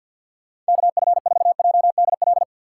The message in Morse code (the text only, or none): KF4JDL